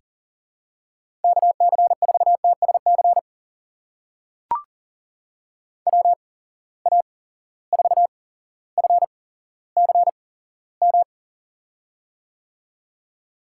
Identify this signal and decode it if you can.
Morse code, heard as a keyed tone, about 40 wpm, spaced as for 17 wpm, 700 Hz. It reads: KC4TSC WA4FCM
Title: USA Call Signs - 6 Characters - 8x Character Spacing - 40wpm